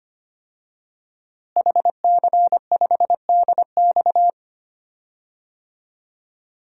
Morse code, heard as a keyed tone, about 25 wpm, 700 Hz. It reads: HC5DX
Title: DXCC-100 Call Signs - 5 Characters - Repeated 3x - 25wpm